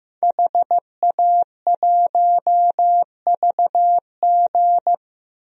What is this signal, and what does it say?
Morse code, heard as a keyed tone, about 15 wpm, 700 Hz. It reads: HA1VG
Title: DXCC-100 Call Signs - 5 Characters - 15wpm